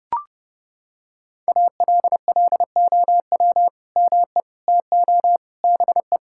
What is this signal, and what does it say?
Morse code, heard as a keyed tone, about 30 wpm, 700 Hz. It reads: ALLOW ME TO BE
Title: Sentences from Top 1000 Words - Review Part 5 - 30wpm